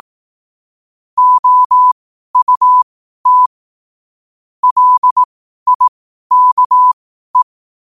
Morse code, heard as a keyed tone, about 18 wpm, spaced as for 12 wpm, 1000 Hz. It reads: OUT LIKE
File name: qotdfiles / 12wpm